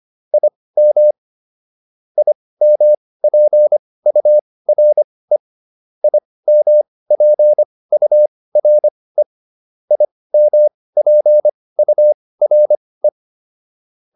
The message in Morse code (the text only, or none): IM IMPURE IMPURE IMPURE